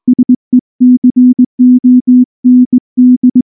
Morse code, heard as a keyed tone, about 19 wpm, 250 Hz. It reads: SECOND
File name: morse / morse-training-20WPM-3